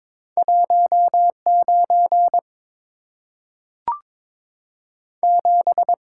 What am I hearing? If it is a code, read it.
Morse code, heard as a keyed tone, about 22 wpm, 700 Hz. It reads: 19 7
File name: practice / 22wpm / Two Letter-Number Combinations - 22wpm